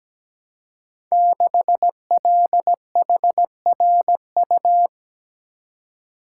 Morse code, heard as a keyed tone, about 17 wpm, 700 Hz. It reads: 6LHRU
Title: Code Groups - 5 Letter:Numbers - 17wpm